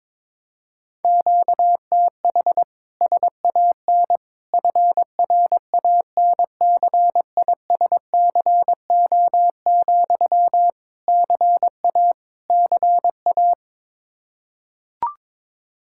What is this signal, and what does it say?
Morse code, heard as a keyed tone, about 22 wpm, 700 Hz. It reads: QTH SAN FRANCISCO, CA CA
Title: K1USN SST QSO Practice 22wpm